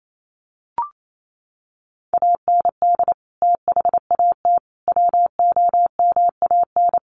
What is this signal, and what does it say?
Morse code, heard as a keyed tone, about 28 wpm, 700 Hz. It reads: AND THAT WOMAN
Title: Sentences from Top 1100 Words 28wpm